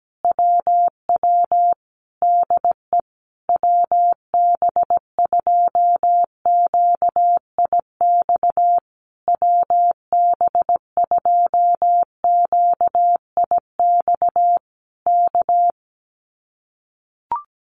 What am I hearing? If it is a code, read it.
Morse code, heard as a keyed tone, about 17 wpm, 700 Hz. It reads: WW DE WB2QIX WB2QIX K
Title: Common QSO Phrases Encore 17wpm